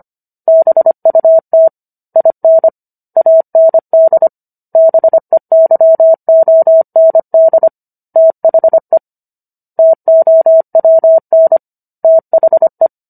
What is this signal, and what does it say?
Morse code, heard as a keyed tone, about 25 wpm, 650 Hz. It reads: BUT IN AND BEYOND THE TOWN THE